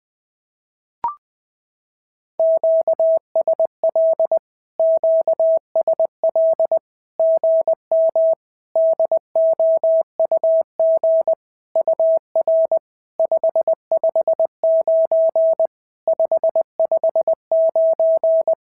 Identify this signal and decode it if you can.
Morse code, heard as a keyed tone, about 20 wpm, 650 Hz. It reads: QSL QSL GM DOUG UR 559 559
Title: SOTA QSO Practice 20wpm